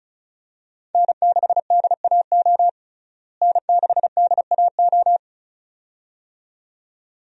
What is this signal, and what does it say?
Morse code, heard as a keyed tone, about 36 wpm, 700 Hz. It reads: N6DAO N6DAO
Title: USA Call Signs - 5 Characters - Repeated 2x - 35wpm